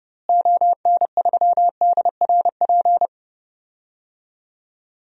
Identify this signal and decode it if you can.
Morse code, heard as a keyed tone, about 30 wpm, 700 Hz. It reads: ON3DRP
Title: DXCC-100 Call Signs - 6 Characters - Repeated 2x - 30wpm